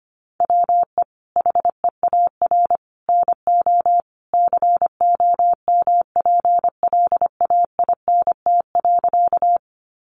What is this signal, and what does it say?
Morse code, heard as a keyed tone, about 25 wpm, 700 Hz. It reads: WE HEAR NO COMPLAINT.
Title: Sentences from Top 1400 Words 25wpm